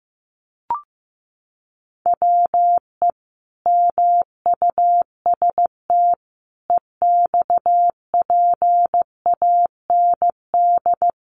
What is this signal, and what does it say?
Morse code, heard as a keyed tone, about 15 wpm, 700 Hz. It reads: WE MUST EXPAND
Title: Sentences from Top 1400 Words 15wpm